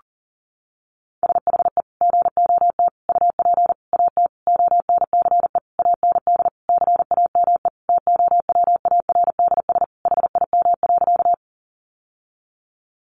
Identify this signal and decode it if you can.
Morse code, heard as a keyed tone, about 40 wpm, 700 Hz. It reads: SHE GOT UP AT ONCE AND CAME TOWARDS HIM.
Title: Sentences from Top 1100 Words 40wpm